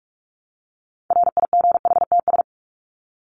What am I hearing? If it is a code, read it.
Morse code, heard as a keyed tone, about 44 wpm, 700 Hz. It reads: RIGHTS
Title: ES:S Suffix - Top 100 Words with Base Words 45wpm